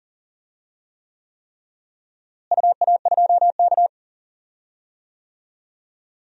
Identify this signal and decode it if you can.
Morse code, heard as a keyed tone, about 39 wpm, 700 Hz. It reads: UA2K
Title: DXCC-100 Call Signs - 4 Characters - Repeated 3x - 40wpm